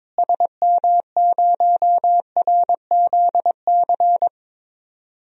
Morse code, heard as a keyed tone, about 22 wpm, 700 Hz. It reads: SM0RZC
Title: DXCC-100 Call Signs - 6 Characters - Repeated 2x - 22wpm